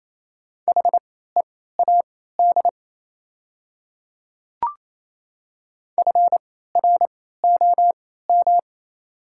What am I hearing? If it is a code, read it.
Morse code, heard as a keyed tone, about 27 wpm, spaced as for 16 wpm, 700 Hz. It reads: HEAD FROM
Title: 3x Character Spacing - Top 300 Words - 28wpm